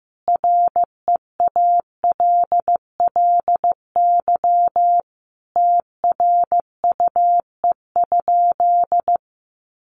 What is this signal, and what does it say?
Morse code, heard as a keyed tone, about 15 wpm, 700 Hz. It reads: REALLY TRUE?
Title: Sentences from Top 600 Words 15wpm